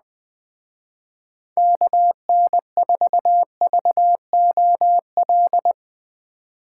Morse code, heard as a keyed tone, about 21 wpm, 700 Hz. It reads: KN4VOL
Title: Speed Racing - USA 2x3 Call Signs 17wpm